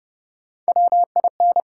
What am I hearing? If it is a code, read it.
Morse code, heard as a keyed tone, about 30 wpm, 700 Hz. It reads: WIN
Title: Three Letter Words - 30wpm